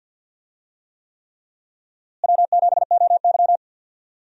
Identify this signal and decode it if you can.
Morse code, heard as a keyed tone, about 52 wpm, 700 Hz. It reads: W7OY
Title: USA Call Signs - 4 Characters - Repeated 3x - 50wpm